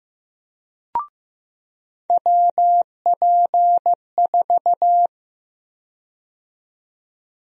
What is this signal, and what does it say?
Morse code, heard as a keyed tone, about 15 wpm, 700 Hz. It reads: WP4
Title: USA Call Sign Prefixes - 15wpm